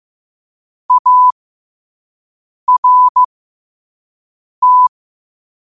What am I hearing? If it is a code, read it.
Morse code, heard as a keyed tone, about 15 wpm, spaced as for 5 wpm, 1000 Hz. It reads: ART
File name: qotdfiles / 5wpm